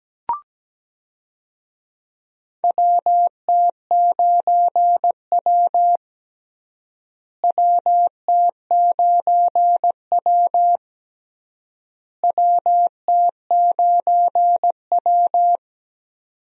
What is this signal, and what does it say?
Morse code, heard as a keyed tone, about 17 wpm, 700 Hz. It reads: WT9W WT9W WT9W
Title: USA Call Signs - 4 Characters - Repeated 3x - 17wpm